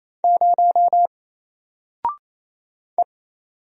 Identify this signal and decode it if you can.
Morse code, heard as a keyed tone, about 28 wpm, 700 Hz. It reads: 0 E
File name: Single-Letter-Number-28wpm